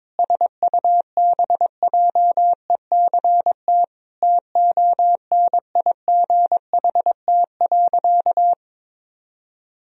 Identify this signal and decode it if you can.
Morse code, heard as a keyed tone, about 22 wpm, 700 Hz. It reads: SUBJECT TONIGHT.
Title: Sentences from Top 1000 Words - Review Part 4 - 22wpm